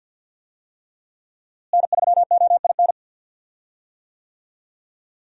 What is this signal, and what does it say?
Morse code, heard as a keyed tone, about 52 wpm, 700 Hz. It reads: N3OIN
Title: USA Call Signs - 5 Characters - Repeated 3x - 50wpm